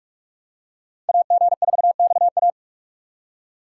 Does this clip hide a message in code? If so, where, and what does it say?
Yes: Morse code from 1.1 to 2.5 s, heard as a keyed tone, about 44 wpm, 700 Hz. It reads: AG4XA